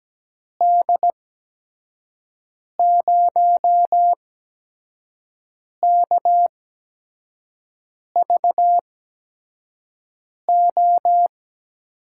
Morse code, heard as a keyed tone, about 17 wpm, spaced as for 5 wpm, 700 Hz. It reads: D0KVO
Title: USA Call Signs - 6 Characters - 8x Character Spacing - 17wpm